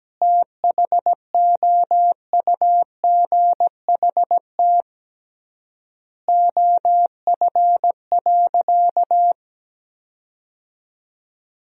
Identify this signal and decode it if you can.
Morse code, heard as a keyed tone, about 17 wpm, 700 Hz. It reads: THOUGHT OF.
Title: 3x Word-Spacing - Sentences from Top 300 Words - 17wpm